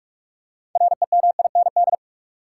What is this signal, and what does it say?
Morse code, heard as a keyed tone, about 46 wpm, 700 Hz. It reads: REMIND
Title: Six Letter Words - 45wpm